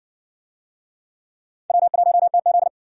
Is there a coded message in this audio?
Yes: Morse code from 1.7 to 2.7 s, heard as a keyed tone, about 60 wpm, 700 Hz. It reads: W1TZ